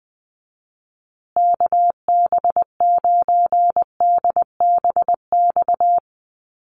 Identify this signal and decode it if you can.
Morse code, heard as a keyed tone, about 20 wpm, 700 Hz. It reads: KB9DBX
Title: USA Call Signs Repeated 3x 20wpm